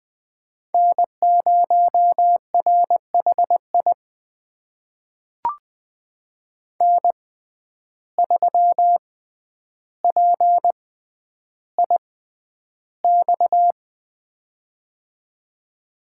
Morse code, heard as a keyed tone, about 20 wpm, spaced as for 10 wpm, 700 Hz. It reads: N0RHI N3PIX
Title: USA Call Signs - 5 Characters - 6x Character Spacing - 20wpm